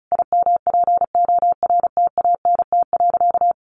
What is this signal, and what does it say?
Morse code, heard as a keyed tone, about 34 wpm, 700 Hz. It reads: IMPORTANT.
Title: Sentences from Top 1200 Words 35wpm